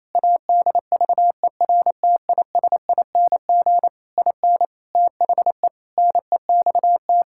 Morse code, heard as a keyed tone, about 28 wpm, 700 Hz. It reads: ADVERTISING IN THE NEXT
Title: Sentences from Top 1000 Words - Review Part 6 - 28wpm